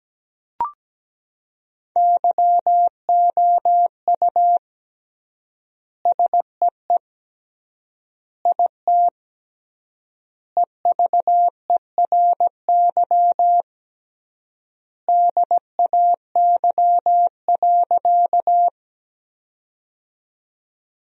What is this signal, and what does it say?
Morse code, heard as a keyed tone, about 17 wpm, 700 Hz. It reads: YOU SEE IT EVERY DAY.
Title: 3x Word-Spacing - Sentences from Top 200 Words - 17wpm